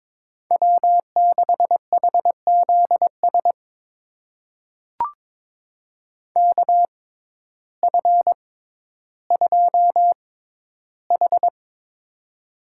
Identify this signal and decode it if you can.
Morse code, heard as a keyed tone, about 22 wpm, spaced as for 12 wpm, 700 Hz. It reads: W6HZS KF2H
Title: USA Call Signs - 6x Character Spacing - 22wpm